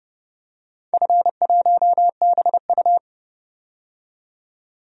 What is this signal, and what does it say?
Morse code, heard as a keyed tone, about 32 wpm, 700 Hz. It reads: F1BU
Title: DXCC-100 Call Signs - 4 Characters - 30wpm